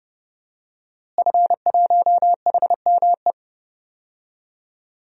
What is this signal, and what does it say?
Morse code, heard as a keyed tone, about 30 wpm, 700 Hz. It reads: F1HME
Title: DXCC-100 Call Signs - 5 Characters - Repeated 3x - 30wpm